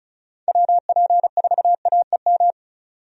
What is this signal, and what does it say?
Morse code, heard as a keyed tone, about 36 wpm, 700 Hz. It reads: WP4AEM
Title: USA Call Signs - 6 Characters - 35wpm